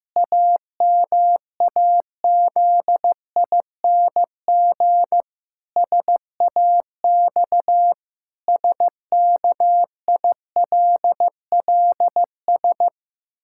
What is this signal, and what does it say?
Morse code, heard as a keyed tone, about 15 wpm, 700 Hz. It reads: AMAZING SAX SKILLS
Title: Pangram Challenge v2 - 15wpm